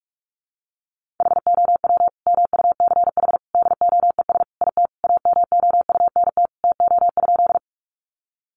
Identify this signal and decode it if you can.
Morse code, heard as a keyed tone, about 44 wpm, 700 Hz. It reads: HOW MUCH DOES IT AMOUNT TO?
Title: Sentences from Top 700 Words 45wpm